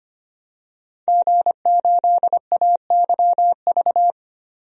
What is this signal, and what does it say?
Morse code, heard as a keyed tone, about 25 wpm, 700 Hz. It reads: G8AYV
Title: DXCC-100 Call Signs - 5 Characters - 25wpm